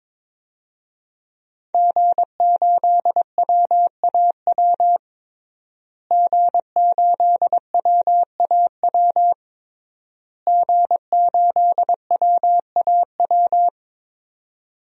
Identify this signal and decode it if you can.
Morse code, heard as a keyed tone, about 22 wpm, 700 Hz. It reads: G8WAW G8WAW G8WAW
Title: DXCC-100 Call Signs - 5 Characters - Repeated 3x - 22wpm